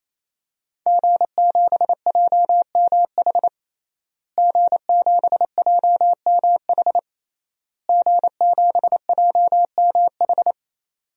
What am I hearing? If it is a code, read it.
Morse code, heard as a keyed tone, about 28 wpm, 700 Hz. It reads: G7JMH G7JMH G7JMH
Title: DXCC-100 Call Signs - 5 Characters - Repeated 3x - 28wpm